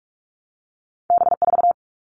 Morse code, heard as a keyed tone, about 46 wpm, 700 Hz. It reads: B4
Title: All Cut Numbers - 2-Digit 45wpm